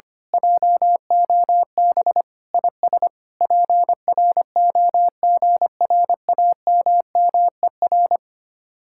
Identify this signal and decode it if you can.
Morse code, heard as a keyed tone, about 24 wpm, 700 Hz. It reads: JOB IS PROGRAMMER
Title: Speed-Racing Common QSO Phrases - 25wpm